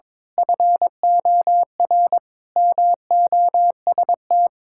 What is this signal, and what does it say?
Morse code, heard as a keyed tone, about 22 wpm, 700 Hz. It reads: FOR MOST
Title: Pangram Challenge v2 - 22wpm